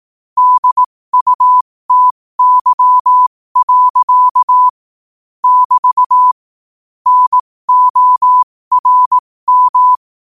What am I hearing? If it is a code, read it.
Morse code, heard as a keyed tone, about 18 wpm, spaced as for 15 wpm, 1000 Hz. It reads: DUTY. <BT> NORM
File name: qotdfiles / 15wpm